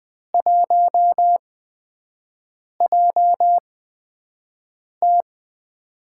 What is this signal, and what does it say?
Morse code, heard as a keyed tone, about 20 wpm, spaced as for 5 wpm, 700 Hz. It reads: 1JT
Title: USA Call Signs - 8x Character Spacing - 20wpm